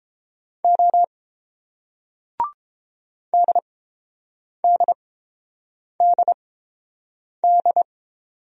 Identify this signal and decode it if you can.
Morse code, heard as a keyed tone, about 30 wpm, spaced as for 7 wpm, 690 Hz. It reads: O DDDD